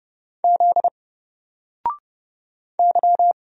Morse code, heard as a keyed tone, about 30 wpm, 700 Hz. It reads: Z Y
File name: Single-Letter-Number-30wpm